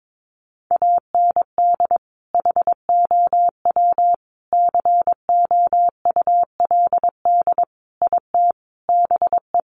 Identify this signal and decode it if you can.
Morse code, heard as a keyed tone, about 22 wpm, 700 Hz. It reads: AND HOW COULD IT BE
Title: Sentences from Top 1200 Words 22wpm